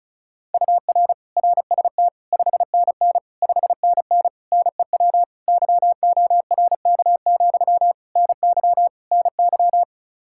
Morse code, heard as a keyed tone, about 36 wpm, 700 Hz. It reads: UR RST 5NN 5NN NEW YORK, NY NY
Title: Practice QSO - Exchange 1 - Beginning - 35wpm